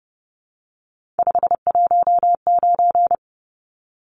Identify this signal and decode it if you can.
Morse code, heard as a keyed tone, about 30 wpm, 700 Hz. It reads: 519
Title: Numbers - 3-Digit 30wpm